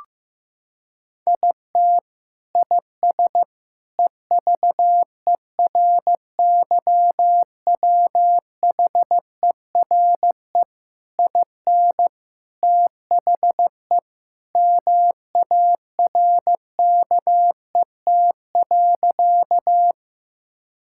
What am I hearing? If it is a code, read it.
Morse code, heard as a keyed tone, about 15 wpm, 700 Hz. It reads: IT IS EVERYWHERE IN THE MARKET.
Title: Sentences from Top 1600 Words 15wpm